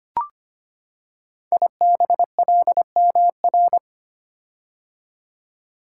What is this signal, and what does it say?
Morse code, heard as a keyed tone, about 25 wpm, 700 Hz. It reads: IBLMR
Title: Code Groups - 5 Letters - 25wpm